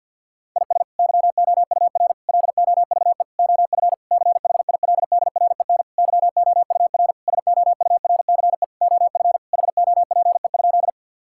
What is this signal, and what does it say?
Morse code, heard as a keyed tone, about 50 wpm, 700 Hz. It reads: IS YOUR LOVE OF CHILDREN YOUR SOURCE OF HOPE?
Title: Sentences from Top 1000 Words - Review Part 5 - 50wpm